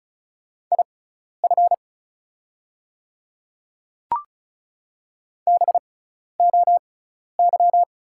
Morse code, heard as a keyed tone, about 36 wpm, spaced as for 12 wpm, 690 Hz. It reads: IF BOY
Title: 6x Character Spacing - Top 500 Words - 35wpm